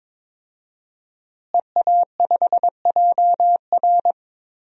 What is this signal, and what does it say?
Morse code, heard as a keyed tone, about 22 wpm, 700 Hz. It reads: EA5JR
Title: DXCC-100 Call Signs - 5 Characters - Repeated 2x - 22wpm